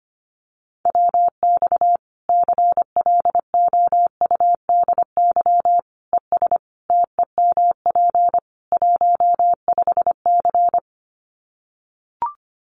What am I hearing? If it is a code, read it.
Morse code, heard as a keyed tone, about 25 wpm, 700 Hz. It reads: WX CLOUDY ES TEMP 15C
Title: Common QSO Phrases 25wpm